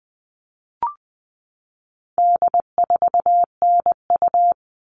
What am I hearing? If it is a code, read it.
Morse code, heard as a keyed tone, about 20 wpm, 700 Hz. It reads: D4NU
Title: All Cut Numbers - 4-Digit 20wpm